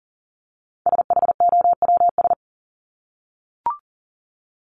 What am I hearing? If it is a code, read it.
Morse code, heard as a keyed tone, about 40 wpm, 700 Hz. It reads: SHOWS
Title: ES:S Suffix - Top 100 Words with Base Words 40wpm